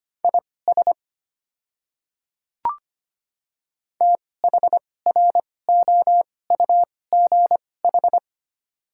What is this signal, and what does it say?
Morse code, heard as a keyed tone, about 25 wpm, spaced as for 18 wpm, 700 Hz. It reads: IS THROUGH
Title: 2x Character Spacing - Top 200 Words - 25wpm